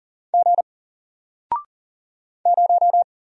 Morse code, heard as a keyed tone, about 40 wpm, 700 Hz. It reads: G 0
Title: Single Character (Letter:Number:Punctuation:Prosign) Alt 40wpm